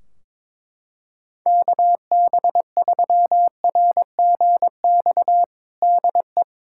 Morse code, heard as a keyed tone, about 22 wpm, 700 Hz. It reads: KB3RGX DE
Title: Practice QSO - Exchange 4 - Ending - 22wpm